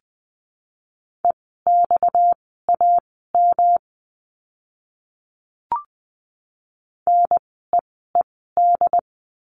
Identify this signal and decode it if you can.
Morse code, heard as a keyed tone, about 20 wpm, spaced as for 14 wpm, 700 Hz. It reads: EXAM NEED
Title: 230 Lesson 37f - Letter x - Words 20wpm